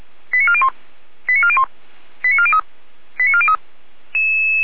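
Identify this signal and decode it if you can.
Five-tone selective call (ZVEI), heard as quick runs of five tones, tones of about 70 ms. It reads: 89401 89401 89403 89403
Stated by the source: ZVEI five-tone call 89361, received on SDR, logged by feuerwehr-orlamuende fire brigade.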